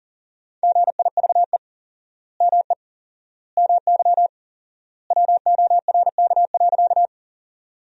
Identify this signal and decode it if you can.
Morse code, heard as a keyed tone, about 40 wpm, 700 Hz. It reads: GIVE ME MY WORK.
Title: 4x Word-Spacing - Sentences from Top 200 Words - 40wpm